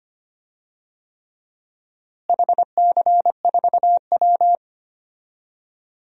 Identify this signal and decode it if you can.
Morse code, heard as a keyed tone, about 25 wpm, 700 Hz. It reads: HC4W
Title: DXCC-100 Call Signs - 4 Characters - Repeated 3x - 25wpm